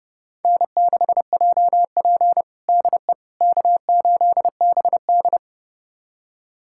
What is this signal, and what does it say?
Morse code, heard as a keyed tone, about 30 wpm, 700 Hz. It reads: N6JP DE K8BD
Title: USA Callsign DE Callsign - 4 Characters - 30wpm